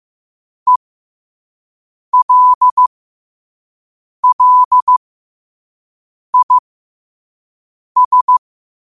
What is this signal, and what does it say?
Morse code, heard as a keyed tone, about 15 wpm, spaced as for 5 wpm, 1000 Hz. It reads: ELLIS
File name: qotdfiles / 5wpm